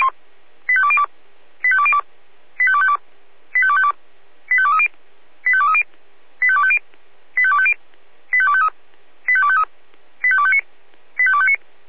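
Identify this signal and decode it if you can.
Five-tone selective call (ZVEI), heard as quick runs of five tones, tones of about 70 ms. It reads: 85202 85202 85252 85252 85230 85230 85250 85250 85253 85253 85259 85259
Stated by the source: ZVEI five-tone selective call received on an SDR.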